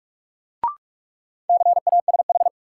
Morse code, heard as a keyed tone, about 46 wpm, 700 Hz. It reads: CASH
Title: Sentences from Top 700 Words - New Word Drill - 45wpm